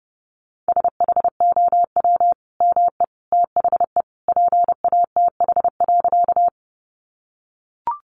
Morse code, heard as a keyed tone, about 30 wpm, 700 Hz. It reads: SHOW ME THE PATH.
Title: Sentences from Top 1400 Words 30wpm